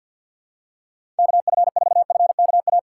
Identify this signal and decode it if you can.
Morse code, heard as a keyed tone, about 50 wpm, 700 Hz. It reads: KF4FKA
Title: USA Call Signs - 6 Characters - 3x Character Spacing - 50wpm